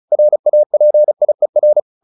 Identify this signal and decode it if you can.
Morse code, heard as a keyed tone, about 34 wpm, 600 Hz. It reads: RAPIER
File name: the old man and the sea / the old man and the sea 1 35wpm